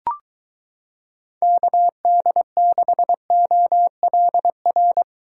Morse code, heard as a keyed tone, about 22 wpm, 700 Hz. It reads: KD6OLR